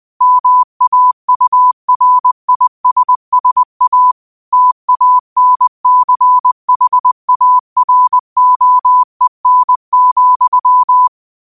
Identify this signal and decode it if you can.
Morse code, heard as a keyed tone, about 20 wpm, 1000 Hz. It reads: MAURISSA TANCHAROEN,
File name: qotdfiles / 20wpm